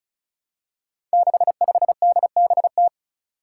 Morse code, heard as a keyed tone, about 36 wpm, 700 Hz. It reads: 65DBT
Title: Five Challenging Letter-Number Combinations 35wpm